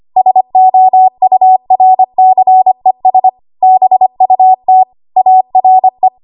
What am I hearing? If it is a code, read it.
Morse code, heard as a keyed tone, about 25 wpm, 750 Hz. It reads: SOURCES BUT ARE
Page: W1AW Web Code Practice Files